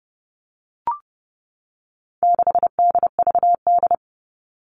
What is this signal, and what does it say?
Morse code, heard as a keyed tone, about 30 wpm, 700 Hz. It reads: 6DVD
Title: All Cut Numbers - 4-Digit 30wpm